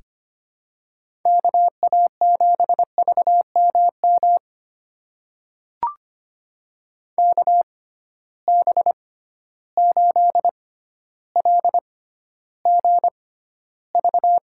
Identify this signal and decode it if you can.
Morse code, heard as a keyed tone, about 25 wpm, spaced as for 13 wpm, 700 Hz. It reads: KA7VMM KB8LGV
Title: USA Call Signs - 6 Characters - 6x Character Spacing - 25wpm